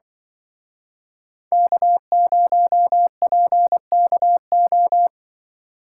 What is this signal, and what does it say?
Morse code, heard as a keyed tone, about 25 wpm, 700 Hz. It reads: K0PKO